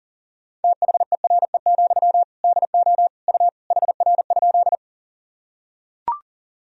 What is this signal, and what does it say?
Morse code, heard as a keyed tone, about 40 wpm, 700 Hz. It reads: THERE, DO U HR?
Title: Sentences from Top 300 Words and Abbreviations 40wpm